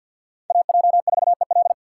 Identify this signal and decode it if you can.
Morse code, heard as a keyed tone, about 51 wpm, 700 Hz. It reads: AJ4EL
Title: USA Call Signs - 6x Character Spacing - 50wpm